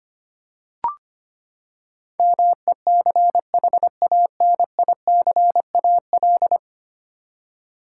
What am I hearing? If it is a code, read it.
Morse code, heard as a keyed tone, about 25 wpm, 700 Hz. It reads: MECHANICAL
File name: Ten Letter Words - 25wpm